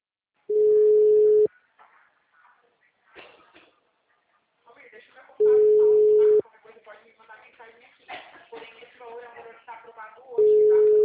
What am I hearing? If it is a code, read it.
Ringback tone (Europe, 425 Hz), with speech.
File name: out- -2006- - - .141